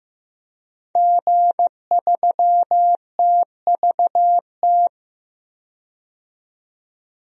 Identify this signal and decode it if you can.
Morse code, heard as a keyed tone, about 15 wpm, 700 Hz. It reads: G3TVT
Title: DXCC-100 Call Signs - 5 Characters - 15wpm